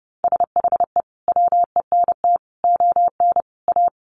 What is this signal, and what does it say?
Morse code, heard as a keyed tone, about 30 wpm, 700 Hz. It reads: SHE WENT ON A
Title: Sentences from Top 1400 Words 30wpm